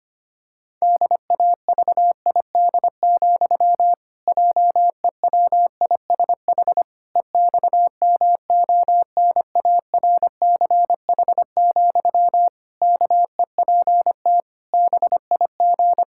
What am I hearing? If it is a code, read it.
Morse code, heard as a keyed tone, about 25 wpm, 700 Hz. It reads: DAVID, JEWISH EXMONARCH, KEPT BIG
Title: Pangram Challenge v2 - 25wpm